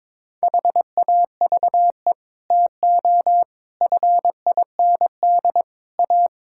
Morse code, heard as a keyed tone, about 22 wpm, 700 Hz. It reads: HAVE TO FIND A
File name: practice / 22wpm / Sets of 6 Words - 22wpm